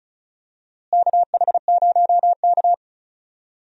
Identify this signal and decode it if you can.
Morse code, heard as a keyed tone, about 36 wpm, 700 Hz. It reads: KH0K
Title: USA Call Signs - 4 Characters - Repeated 2x - 35wpm